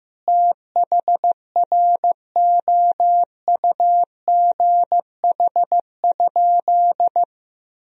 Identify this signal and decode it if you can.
Morse code, heard as a keyed tone, about 15 wpm, 700 Hz. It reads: THROUGH?
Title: Sentences from Top 1000 Words - Review Part 2 - 15wpm